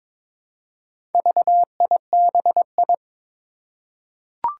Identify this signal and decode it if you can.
Morse code, heard as a keyed tone, about 22 wpm, 700 Hz. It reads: VIBI
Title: Four Challenging Letter-Number Combinations 22wpm